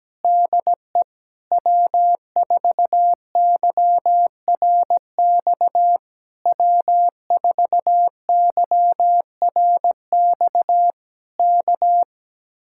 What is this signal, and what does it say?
Morse code, heard as a keyed tone, about 17 wpm, 700 Hz. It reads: DE W4YRX W4YRX K
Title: Speed-Racing Common QSO Phrases - 17wpm